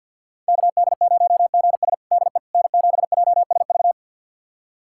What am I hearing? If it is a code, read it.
Morse code, heard as a keyed tone, about 53 wpm, 700 Hz. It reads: KD0GS DE N7JSV
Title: USA Callsign DE Callsign - 5 Characters - 50wpm